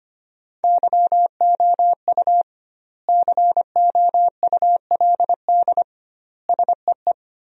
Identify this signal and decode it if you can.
Morse code, heard as a keyed tone, about 25 wpm, 700 Hz. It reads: YOU COULD SEE
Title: 2x Word-Spacing - Sentences from Top 500 Words - 25wpm